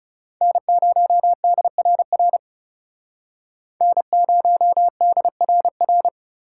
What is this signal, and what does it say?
Morse code, heard as a keyed tone, about 37 wpm, 700 Hz. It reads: N0DRR N0DRR
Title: Speed Racing - USA 1x3 Call Signs 30wpm